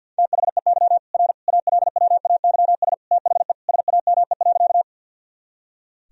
Morse code, heard as a keyed tone, about 52 wpm, 700 Hz. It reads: THEY R ALWAYS THE SAME.
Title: Sentences from Top 300 Words and Abbreviations 50wpm